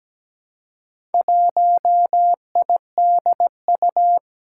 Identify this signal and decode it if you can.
Morse code, heard as a keyed tone, about 17 wpm, 700 Hz. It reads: 1IDU
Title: Four Challenging Letter-Number Combinations 17wpm